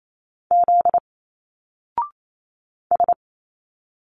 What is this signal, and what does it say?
Morse code, heard as a keyed tone, about 28 wpm, 700 Hz. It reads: Z S
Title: Single Letters 28wpm